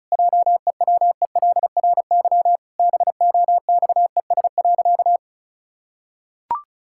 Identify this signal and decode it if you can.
Morse code, heard as a keyed tone, about 36 wpm, 700 Hz. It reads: JEWELRY BOXES.
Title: Pangram Challenge v2 - 35wpm